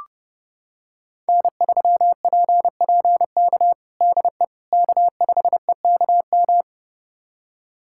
Morse code, heard as a keyed tone, about 30 wpm, 700 Hz. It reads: N3PPK DE K5EKM
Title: USA Callsign DE Callsign - 5 Characters - 30wpm